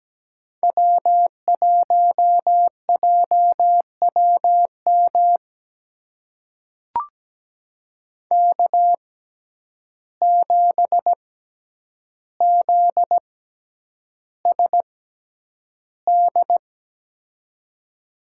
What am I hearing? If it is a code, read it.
Morse code, heard as a keyed tone, about 17 wpm, spaced as for 9 wpm, 700 Hz. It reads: W1JWM K7ZSD
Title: USA Call Signs - 6x Character Spacing - 17wpm